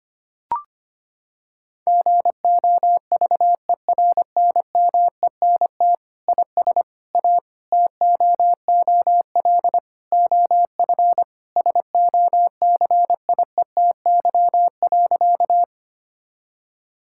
Morse code, heard as a keyed tone, about 25 wpm, 700 Hz. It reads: GOVERNMENT IS A TOOL OF SOCIETY.
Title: Sentences from Top 1000 Words - Review Part 6 - 25wpm